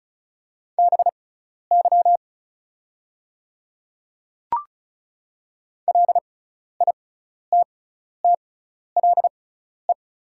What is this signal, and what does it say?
Morse code, heard as a keyed tone, about 36 wpm, spaced as for 12 wpm, 700 Hz. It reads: BY LITTLE
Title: 6x Character Spacing - Top 200 Words - 35wpm